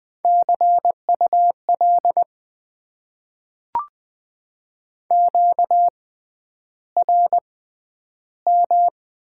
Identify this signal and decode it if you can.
Morse code, heard as a keyed tone, about 20 wpm, spaced as for 10 wpm, 700 Hz. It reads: CUL QRM
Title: Top 90 QSO Elements - 6x Character Spacing - 20wpm